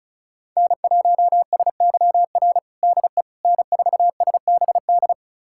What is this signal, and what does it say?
Morse code, heard as a keyed tone, about 36 wpm, 700 Hz. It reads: N1SYR DE N4SBD
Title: USA Callsign DE Callsign - 5 Characters - 35wpm